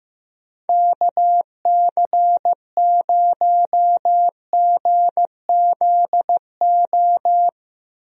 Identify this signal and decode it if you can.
Morse code, heard as a keyed tone, about 15 wpm, 700 Hz. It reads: KC0GZO
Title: USA Call Signs - 6 Characters - Repeated 2x - 15wpm